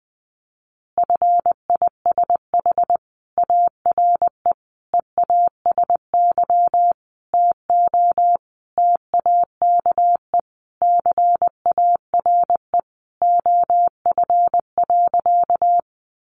Morse code, heard as a keyed tone, about 20 wpm, 700 Hz. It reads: FISH ARE EASY TO TAKE CARE OF.
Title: Easy Phrases 20wpm